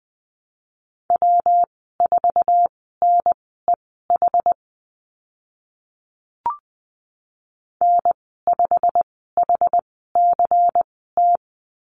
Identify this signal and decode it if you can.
Morse code, heard as a keyed tone, about 20 wpm, spaced as for 14 wpm, 700 Hz. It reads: W4NEH N5HCT
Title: 154 Lesson 25g - Letter p - Call Signs 20wpm